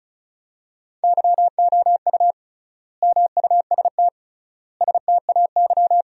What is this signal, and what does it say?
Morse code, heard as a keyed tone, about 36 wpm, 700 Hz. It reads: YOU MUST STAY
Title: 3x Word-Spacing - Sentences from Top 500 Words - 35wpm